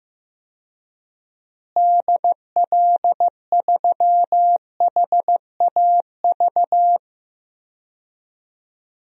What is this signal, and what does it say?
Morse code, heard as a keyed tone, about 15 wpm, 700 Hz. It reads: DL3HAV
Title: DXCC-100 Call Signs - 6 Characters - Repeated 2x - 15wpm